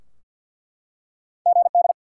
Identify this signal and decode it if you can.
Morse code, heard as a keyed tone, about 48 wpm, 700 Hz. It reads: GD
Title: Top 180 QSO Elements 50wpm